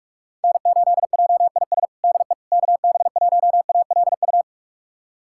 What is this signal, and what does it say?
Morse code, heard as a keyed tone, about 41 wpm, 700 Hz. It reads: N8JIS DE KB1ALU
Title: USA Callsign DE Callsign 45wpm